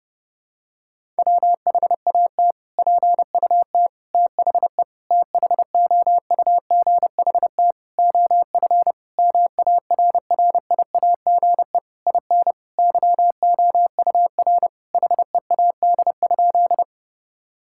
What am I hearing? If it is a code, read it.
Morse code, heard as a keyed tone, about 30 wpm, 700 Hz. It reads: WHAT PUT THE THOUGHT OF MARRIAGE IN YOUR HEAD?
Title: Sentences from Top 1000 Words - Review Part 5 - 30wpm